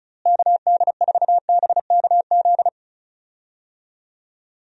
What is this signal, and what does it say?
Morse code, heard as a keyed tone, about 36 wpm, 700 Hz. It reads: KD4BKZ
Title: USA Call Signs - 6 Characters - Repeated 3x - 35wpm